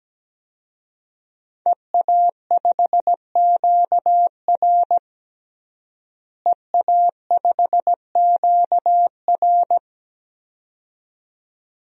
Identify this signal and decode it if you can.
Morse code, heard as a keyed tone, about 17 wpm, 700 Hz. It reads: EA5QR EA5QR
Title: DXCC-100 Call Signs - 5 Characters - Repeated 2x - 17wpm